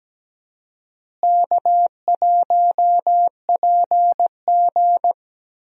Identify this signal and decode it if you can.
Morse code, heard as a keyed tone, about 17 wpm, 700 Hz. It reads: K1PG
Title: USA Call Signs - 4 Characters - Repeated 3x - 17wpm